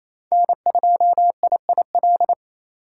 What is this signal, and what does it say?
Morse code, heard as a keyed tone, about 28 wpm, 700 Hz. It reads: N2IIL
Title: USA Call Signs - 5 Characters - 2x Character Spacing - 28wpm